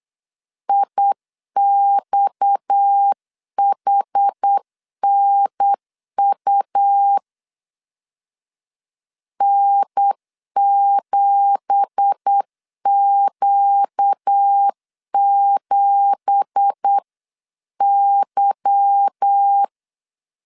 Morse code, heard as a keyed tone, about 8 wpm, 800 Hz. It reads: IXHNU N7Q7Y
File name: CW / DET / 1 9 1500C S A 8wpm 5637